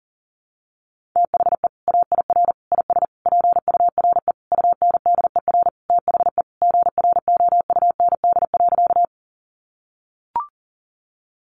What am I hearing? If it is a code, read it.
Morse code, heard as a keyed tone, about 40 wpm, 700 Hz. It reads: THE AIR IS PURE UNDER THE GROUND.
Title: Sentences from Top 1500 Words 40wpm